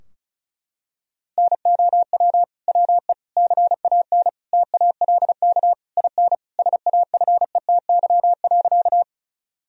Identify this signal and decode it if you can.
Morse code, heard as a keyed tone, about 36 wpm, 700 Hz. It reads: NOW WE CAN TALK IN SAFETY.
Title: Sentences from Top 1000 Words - Review Part 4 - 35wpm